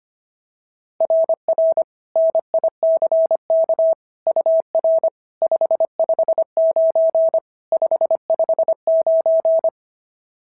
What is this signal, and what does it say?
Morse code, heard as a keyed tone, about 24 wpm, 650 Hz. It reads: RR NICK UR 559 559
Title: SOTA QSO Practice 25wpm